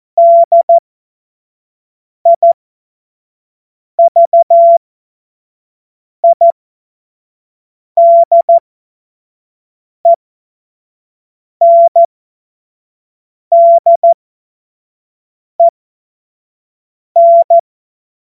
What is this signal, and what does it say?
Morse code, heard as a keyed tone, about 14 wpm, spaced as for 5 wpm, 680 Hz. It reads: DIVIDENDEN
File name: Witz-15-Juli-2025-25-F70